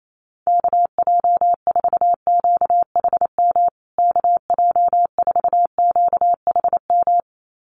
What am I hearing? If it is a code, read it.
Morse code, heard as a keyed tone, about 28 wpm, 700 Hz. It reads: KJ4QHM KJ4QHM
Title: USA Call Signs Repeated 2x 28wpm